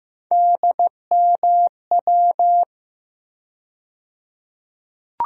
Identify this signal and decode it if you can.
Morse code, heard as a keyed tone, about 15 wpm, 700 Hz. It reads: DMW
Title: USA Call Sign Postfixes - 3 Letters 15wpm